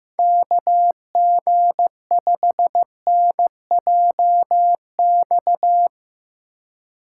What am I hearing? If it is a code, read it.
Morse code, heard as a keyed tone, about 16 wpm, 700 Hz. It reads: KG5NJX
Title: Speed Racing - USA 2x3 Call Signs 15wpm